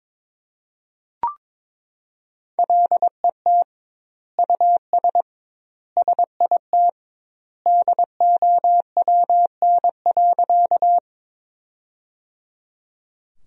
Morse code, heard as a keyed tone, about 22 wpm, 700 Hz. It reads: LET US SIT DOWN.
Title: 2x Word-Spacing - Sentences from Top 500 Words - 22wpm